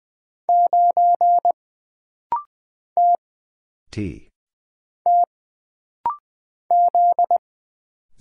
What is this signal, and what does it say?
Morse code, heard as a keyed tone, about 20 wpm, 700 Hz. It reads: 9 T T Z
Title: Single Letter-Number - Rapid-Fire 20wpm